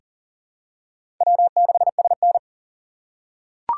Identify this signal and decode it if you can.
Morse code, heard as a keyed tone, about 40 wpm, 700 Hz. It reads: W6SN
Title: USA Call Signs - 4 Characters - 2x Character Spacing - 40wpm